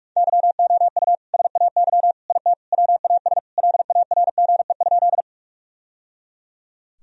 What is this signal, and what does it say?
Morse code, heard as a keyed tone, about 41 wpm, 700 Hz. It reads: YOU SAY IT WAS LARGE?
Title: Sentences from Top 1000 Words - Review Part 1 - 45wpm